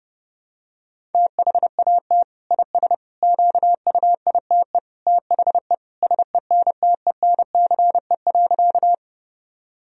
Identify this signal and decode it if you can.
Morse code, heard as a keyed tone, about 30 wpm, 700 Hz. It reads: THAT IS QUITE THE SENTENCE.
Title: Sentences from Top 1000 Words - Review Part 1 - 30wpm